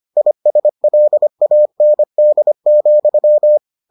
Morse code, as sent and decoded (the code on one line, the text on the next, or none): .. ... .-.. .- -. -.. --..--
ISLAND,